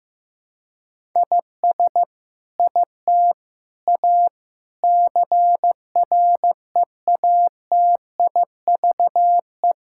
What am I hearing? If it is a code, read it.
Morse code, heard as a keyed tone, about 15 wpm, 700 Hz. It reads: IS IT A CREATIVE
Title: Sentences from Top 1000 Words - Review Part 5 - 15wpm